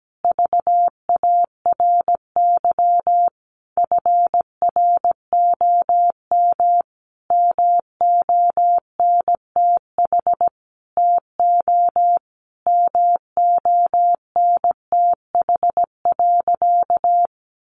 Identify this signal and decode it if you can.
Morse code, heard as a keyed tone, about 17 wpm, 700 Hz. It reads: VARY FROM MONTH TO MONTH.
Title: Sentences from Top 1200 Words 17wpm